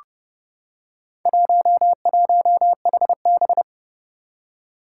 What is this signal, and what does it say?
Morse code, heard as a keyed tone, about 30 wpm, 700 Hz. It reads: 11HB
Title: Four Challenging Letter-Number Combinations 30wpm